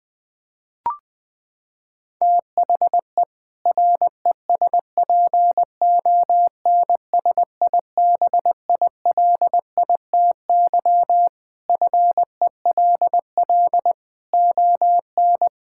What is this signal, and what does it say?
Morse code, heard as a keyed tone, about 20 wpm, 700 Hz. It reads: THE RESPONSIBILITY FELL ON
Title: Sentences with 2-Word Phrases - 20wpm